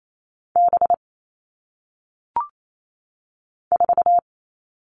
Morse code, heard as a keyed tone, about 28 wpm, 700 Hz. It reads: B 4
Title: All Cut Numbers - 1-Digit 28wpm